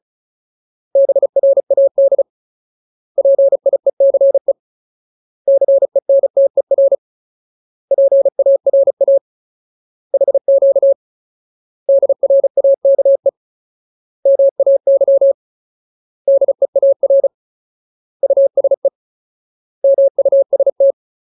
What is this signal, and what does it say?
Morse code, heard as a keyed tone, about 34 wpm, 550 Hz. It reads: BRAD PIECE CENTER PARA HQ DRAKE MAY DEAR USE MUST
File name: Ultimate Word List / 35WPM Ultimate